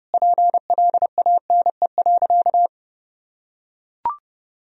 Morse code, heard as a keyed tone, about 30 wpm, 700 Hz. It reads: PLANE.
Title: Sentences from Top 1000 Words - Review Part 3 - 30wpm